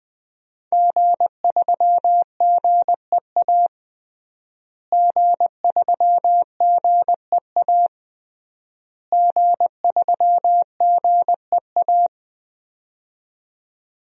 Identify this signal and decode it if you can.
Morse code, heard as a keyed tone, about 21 wpm, 700 Hz. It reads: G3GEA G3GEA G3GEA
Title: DXCC-100 Call Signs - 5 Characters - Repeated 3x - 20wpm